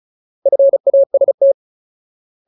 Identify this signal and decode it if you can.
Morse code, heard as a keyed tone, about 34 wpm, 550 Hz. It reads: FAST
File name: Ultimate Word List / 35WPM Ultimate